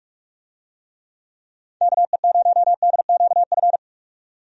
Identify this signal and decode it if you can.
Morse code, heard as a keyed tone, about 42 wpm, 700 Hz. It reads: KE0DQF
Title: USA Call Signs - 6 Characters - 45wpm